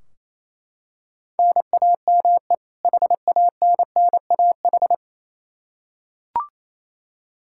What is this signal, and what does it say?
Morse code, heard as a keyed tone, about 28 wpm, 700 Hz. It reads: NAME HANNAH
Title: Practice QSO - Exchange 1 - Beginning - 28wpm